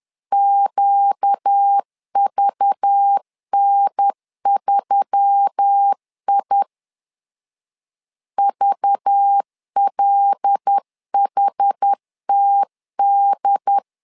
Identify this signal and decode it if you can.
Morse code, heard as a keyed tone, about 10 wpm, 800 Hz. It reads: QVN3I VLHTD
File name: LEZIONI / 5 LBVC / 5 LBV 10wpm 2248